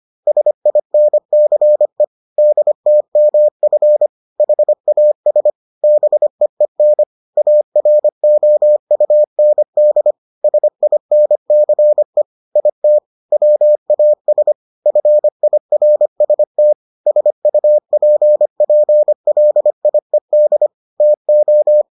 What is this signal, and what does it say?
Morse code, heard as a keyed tone, about 25 wpm, 600 Hz. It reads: SINCE DTMF HAS BEEN AROUND SINCE IT WAS FIRST SUPPLIED TO